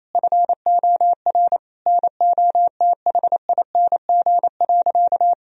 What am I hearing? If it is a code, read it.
Morse code, heard as a keyed tone, about 27 wpm, 700 Hz. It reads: FOR NOTHING.
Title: Sentences from Top 1000 Words - Review Part 3 - 28wpm